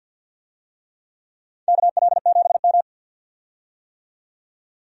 Morse code, heard as a keyed tone, about 51 wpm, 700 Hz. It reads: KL7M
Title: DXCC-100 Call Signs - 4 Characters - 50wpm